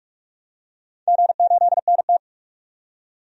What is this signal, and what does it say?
Morse code, heard as a keyed tone, about 46 wpm, 700 Hz. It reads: G8NT
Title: DXCC-100 Call Signs - 4 Characters - 45wpm